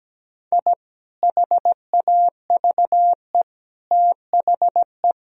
Morse code, heard as a keyed tone, about 17 wpm, 700 Hz. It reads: I HAVE THE
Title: Sentences from Top 1000 Words - Review Part 2 - 17wpm